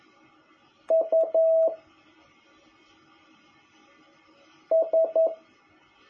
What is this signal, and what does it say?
Morse code, heard as a keyed tone, about 11 wpm, 650 Hz. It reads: U S
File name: jh8cbh / 2021cw